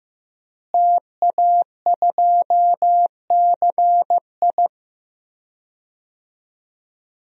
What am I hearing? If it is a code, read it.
Morse code, heard as a keyed tone, about 15 wpm, 700 Hz. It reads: TA2CI
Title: DXCC-100 Call Signs - 5 Characters - Repeated 2x - 15wpm